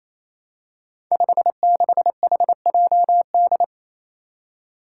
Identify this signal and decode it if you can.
Morse code, heard as a keyed tone, about 28 wpm, 700 Hz. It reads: 56HJD
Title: Five Challenging Letter-Number Combinations 28wpm